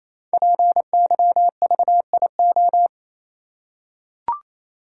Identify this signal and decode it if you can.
Morse code, heard as a keyed tone, about 27 wpm, 700 Hz. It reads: PYVIO
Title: Code Groups - 5 Letter:Numbers - 28wpm